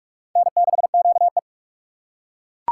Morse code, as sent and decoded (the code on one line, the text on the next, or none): -. -.... --.- .
N6QE